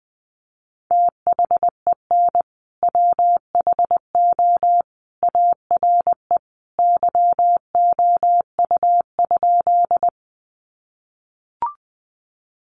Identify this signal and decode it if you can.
Morse code, heard as a keyed tone, about 20 wpm, 700 Hz. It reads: THEN WHO ARE YOU?
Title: Sentences from Top 100 Words 20wpm